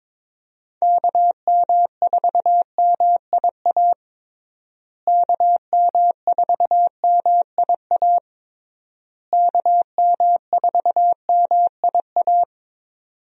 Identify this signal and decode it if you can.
Morse code, heard as a keyed tone, about 21 wpm, 700 Hz. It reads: KM4MIA KM4MIA KM4MIA
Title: USA Call Signs - 6 Characters - Repeated 3x - 22wpm